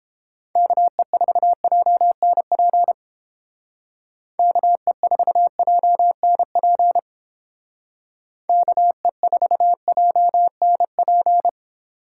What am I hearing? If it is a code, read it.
Morse code, heard as a keyed tone, about 29 wpm, 700 Hz. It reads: KE4JNP KE4JNP KE4JNP